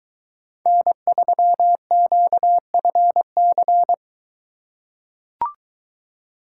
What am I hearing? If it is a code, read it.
Morse code, heard as a keyed tone, about 21 wpm, 700 Hz. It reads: N3QFC